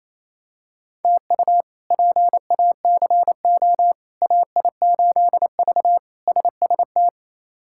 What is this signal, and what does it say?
Morse code, heard as a keyed tone, about 28 wpm, 700 Hz. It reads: TU PACO AI8V SST
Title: K1USN SST QSO Practice - 28wpm